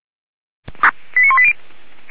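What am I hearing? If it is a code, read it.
Five-tone selective call (ZVEI), heard as quick runs of five tones, tones of about 70 ms. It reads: 89180
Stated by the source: ZVEI five-tone selective call received with an SDR.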